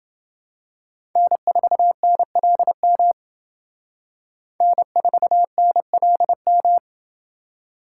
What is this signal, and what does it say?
Morse code, heard as a keyed tone, about 29 wpm, 700 Hz. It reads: N4NLM N4NLM